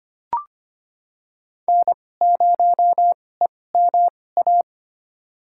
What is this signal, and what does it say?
Morse code, heard as a keyed tone, about 25 wpm, spaced as for 18 wpm, 700 Hz. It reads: N0EMA
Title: USA Call Signs - 5 Characters - 2x Character Spacing - 25wpm